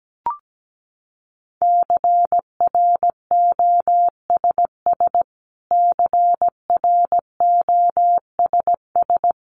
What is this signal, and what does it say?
Morse code, heard as a keyed tone, about 17 wpm, 700 Hz. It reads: CROSS CROSS
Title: Top 500 Words Repeated 17wpm